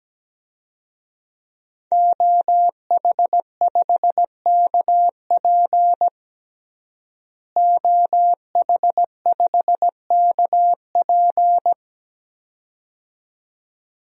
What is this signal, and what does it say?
Morse code, heard as a keyed tone, about 17 wpm, 700 Hz. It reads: OH5KP OH5KP
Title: DXCC-100 Call Signs - 5 Characters - Repeated 2x - 17wpm